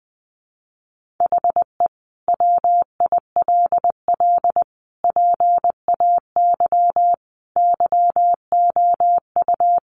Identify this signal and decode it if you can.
Morse code, heard as a keyed tone, about 20 wpm, 700 Hz. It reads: HE WILL PAY YOU
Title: Sentences from Top 1400 Words 20wpm